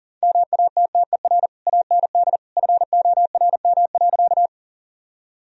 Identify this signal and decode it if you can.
Morse code, heard as a keyed tone, about 40 wpm, 700 Hz. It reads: MATTER AND FORM.
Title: Sentences from Top 1000 Words - Review Part 1 - 40wpm